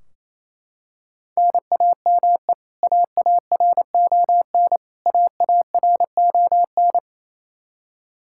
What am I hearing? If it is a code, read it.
Morse code, heard as a keyed tone, about 28 wpm, 700 Hz. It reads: NAME AARON AARON
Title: Practice QSO - Exchange 1 - Beginning - 28wpm